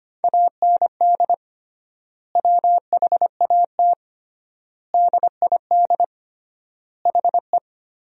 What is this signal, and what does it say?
Morse code, heard as a keyed tone, about 25 wpm, 700 Hz. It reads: AND WHAT DID HE
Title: 3x Word-Spacing - Sentences from Top 200 Words - 25wpm